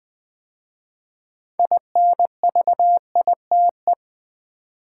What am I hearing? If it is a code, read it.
Morse code, heard as a keyed tone, about 20 wpm, 700 Hz. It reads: INVITE
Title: 6 Letter Words - First 3 Letters Given - 20wpm